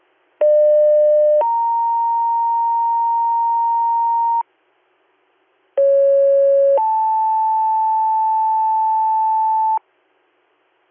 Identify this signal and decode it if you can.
Two-tone page, heard as a one-second tone, then a three-second tone, about 600 then 930 Hz, about 550 then 880 Hz. It is heard from 0.4 to 9.8 s.